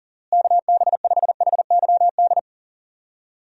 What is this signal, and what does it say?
Morse code, heard as a keyed tone, about 39 wpm, 700 Hz. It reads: KB5HYD
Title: USA Call Signs - 6 Characters - Repeated 2x - 40wpm